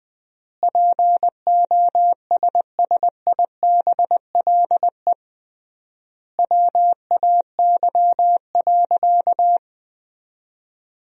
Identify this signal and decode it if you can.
Morse code, heard as a keyed tone, about 21 wpm, 700 Hz. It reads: POSSIBLE WAY.
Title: 3x Word-Spacing - Sentences from Top 500 Words - 20wpm